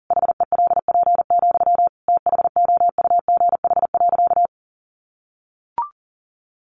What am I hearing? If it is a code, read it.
Morse code, heard as a keyed tone, about 40 wpm, 700 Hz. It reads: HELP, THOUGH.
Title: Sentences from Top 1100 Words 40wpm